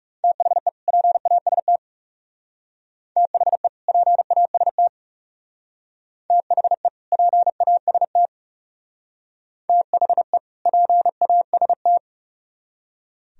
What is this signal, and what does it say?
Morse code, heard as a keyed tone, about 39 wpm, 700 Hz. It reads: THE PAST THE PAST THE PAST THE PAST